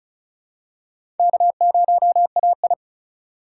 Morse code, heard as a keyed tone, about 34 wpm, 700 Hz. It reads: K0AI
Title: USA Call Signs - 3x Character Spacing - 35wpm